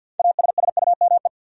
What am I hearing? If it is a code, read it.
Morse code, heard as a keyed tone, about 48 wpm, 700 Hz. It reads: ASSUME
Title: Sentences from Top 1500 Words - New Word Drill - 50wpm